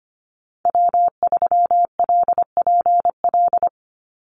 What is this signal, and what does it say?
Morse code, heard as a keyed tone, about 25 wpm, 700 Hz. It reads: W3LPL
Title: Call Signs Repeated 3x - 25wpm